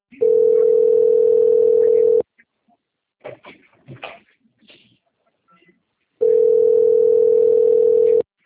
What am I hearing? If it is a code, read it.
Ringback tone (North America), 440 + 480 Hz.